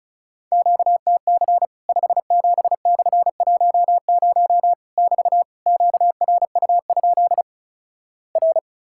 Morse code, heard as a keyed tone, about 36 wpm, 700 Hz. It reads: QTC 57/10 <BT> QRU? R
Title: Worked All Europe DX Contest - QTC Exchange - 35wpm